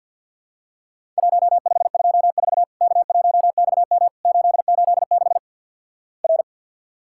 Morse code, heard as a keyed tone, about 50 wpm, 700 Hz. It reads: 1524 K1XM 886 R
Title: Worked All Europe DX Contest - QTC Exchange - 50wpm